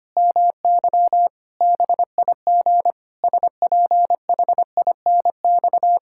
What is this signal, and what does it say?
Morse code, heard as a keyed tone, about 25 wpm, 700 Hz. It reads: MY BIG SPHINX
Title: Pangram Challenge v2 - 25wpm